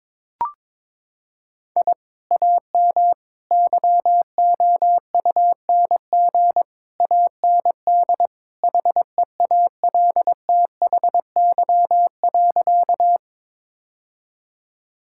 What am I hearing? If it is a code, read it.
Morse code, heard as a keyed tone, about 21 wpm, 700 Hz. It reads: I AM YOUNG AND HEALTHY.
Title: Sentences from Top 1000 Words - Review Part 5 - 22wpm